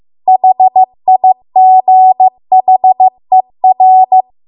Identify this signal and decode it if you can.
Morse code, heard as a keyed tone, about 15 wpm, 750 Hz. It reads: HIGHER